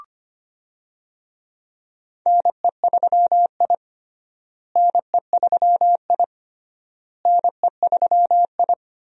Morse code, heard as a keyed tone, about 25 wpm, 700 Hz. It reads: NE3I NE3I NE3I
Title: USA Call Signs - 4 Characters - Repeated 3x - 25wpm